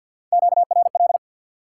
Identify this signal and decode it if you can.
Morse code, heard as a keyed tone, about 53 wpm, 700 Hz. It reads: QRL
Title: Top 180 QSO Elements 50wpm